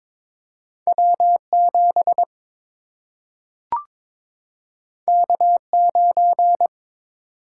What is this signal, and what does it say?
Morse code, heard as a keyed tone, about 22 wpm, 700 Hz. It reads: W 7 K 9
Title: Single-Letter Single-Number 22wpm